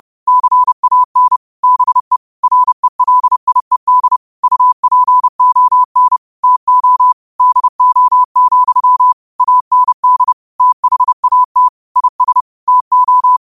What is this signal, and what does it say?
Morse code, heard as a keyed tone, about 30 wpm, 1000 Hz. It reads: CAN BE RELIED UPON TO DO, AND THAT IS TO
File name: qotdfiles / 30wpm